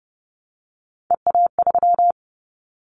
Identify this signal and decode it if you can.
Morse code, heard as a keyed tone, about 30 wpm, 700 Hz. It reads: EA3
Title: Callsign Prefixes for Top 25 DXCC - Repeated 3x - 30wpm